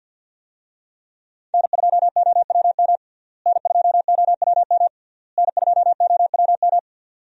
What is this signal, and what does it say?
Morse code, heard as a keyed tone, about 48 wpm, 700 Hz. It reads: N2OWM N2OWM N2OWM
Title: USA Call Signs - 5 Characters - Repeated 3x - 50wpm